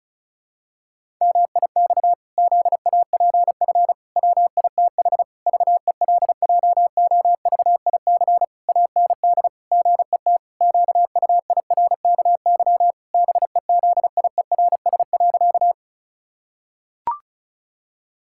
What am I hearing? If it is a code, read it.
Morse code, heard as a keyed tone, about 34 wpm, 700 Hz. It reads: MIX ZAPF WITH VELJOVIC AND GET QUIRKY BEZIERS.
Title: Pangram Challenge v2 - 35wpm